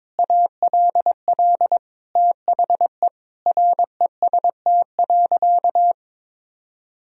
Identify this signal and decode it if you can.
Morse code, heard as a keyed tone, about 22 wpm, 700 Hz. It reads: ALL THE REST.
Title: Sentences from Top 1000 Words - Review Part 1 - 22wpm